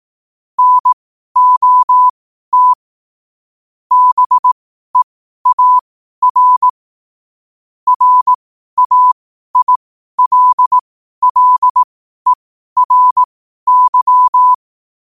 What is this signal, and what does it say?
Morse code, heard as a keyed tone, about 18 wpm, spaced as for 12 wpm, 1000 Hz. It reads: NOT BEAR RAILLERY
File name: qotdfiles / 12wpm